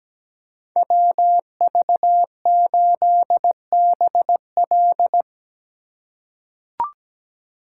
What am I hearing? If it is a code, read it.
Morse code, heard as a keyed tone, about 17 wpm, 700 Hz. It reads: WV8BL
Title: USA Call Signs - 8x Character Spacing - 17wpm